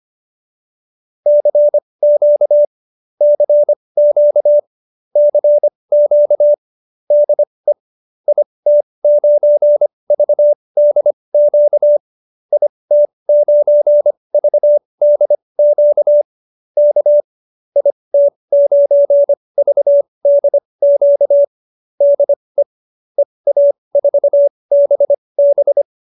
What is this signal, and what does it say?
Morse code, heard as a keyed tone, about 25 wpm, spaced as for 20 wpm, 590 Hz. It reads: CQ CQ CQ DE IT9VDQ IT9VDQ K IT9VDQ DE EA4BB
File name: qso215b 20